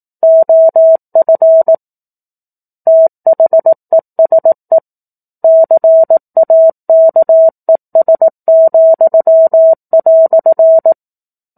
Morse code, heard as a keyed tone, about 18 wpm, 650 Hz. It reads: OF THESE CAKES,"
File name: 04-Chapter-04 18 WPM